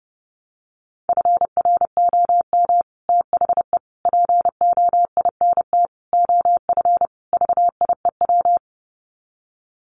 Morse code, heard as a keyed tone, about 30 wpm, 700 Hz. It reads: FROM THE POINT OF VIEW
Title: Sets of 5 Words 30wpm